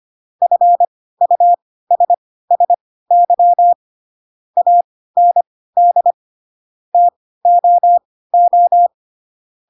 Morse code, heard as a keyed tone, about 25 wpm, spaced as for 16 wpm, 700 Hz. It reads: FUSSY AND TOO
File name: Pinocchio / Pinocchio ch7 10 25 ik1hge